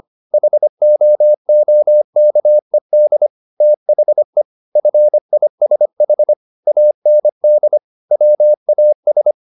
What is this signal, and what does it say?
Morse code, heard as a keyed tone, about 25 wpm, 600 Hz. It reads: HOOKED THE FISH AND WAS